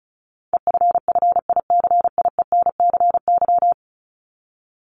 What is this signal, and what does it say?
Morse code, heard as a keyed tone, about 34 wpm, 700 Hz. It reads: EFFICIENCY
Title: Top 1501-1750 Words 35wpm